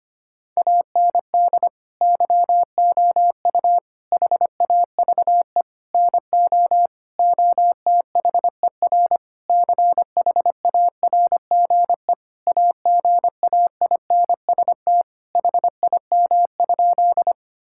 Morse code, heard as a keyed tone, about 25 wpm, 700 Hz. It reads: AND YOU HAVE NO OTHER CHARGE AGAINST HIM?
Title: Sentences from Top 1000 Words - Review Part 6 - 25wpm